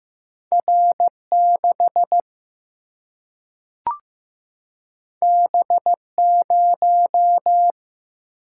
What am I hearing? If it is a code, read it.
Morse code, heard as a keyed tone, about 15 wpm, 700 Hz. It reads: R 6 B 0
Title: Single-Letter Single-Number 15wpm